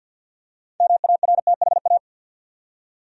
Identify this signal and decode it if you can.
Morse code, heard as a keyed tone, about 50 wpm, 700 Hz. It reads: MARTHA
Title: USA Common Names - 50wpm